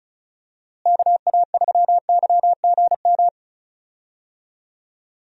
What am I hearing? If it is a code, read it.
Morse code, heard as a keyed tone, about 36 wpm, 700 Hz. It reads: KA3YGM
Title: USA Call Signs - 6 Characters - Repeated 3x - 35wpm